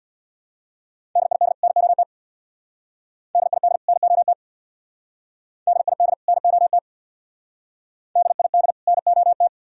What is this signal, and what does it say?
Morse code, heard as a keyed tone, about 61 wpm, 700 Hz. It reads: DID NOT DID NOT DID NOT DID NOT
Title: Speed-Racing - Sets of 2 Words 50wpm